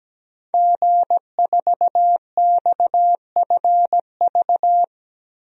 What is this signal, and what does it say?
Morse code, heard as a keyed tone, about 17 wpm, 700 Hz. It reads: G4XFV
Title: DXCC-100 Call Signs - 5 Characters - Repeated 3x - 17wpm